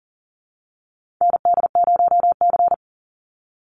Morse code, heard as a keyed tone, about 40 wpm, 700 Hz. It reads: ND0C
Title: NA SOTA Chaser Callsigns 40wpm